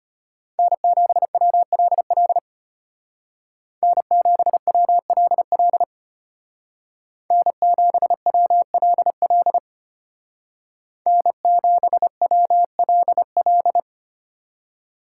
Morse code, heard as a keyed tone, about 32 wpm, 700 Hz. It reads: N7WLL N7WLL N7WLL N7WLL